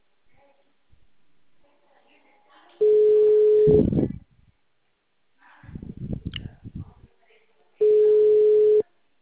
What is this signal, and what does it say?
Ringback tone (Europe, 425 Hz), with speech.
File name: out- -2015- - - .49674